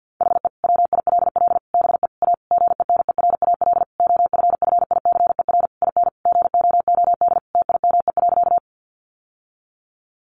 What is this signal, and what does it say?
Morse code, heard as a keyed tone, about 50 wpm, 700 Hz. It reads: HE WILL BE A GENERAL OFFICER IN GOOD TIME.
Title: Sentences from Top 1200 Words 50wpm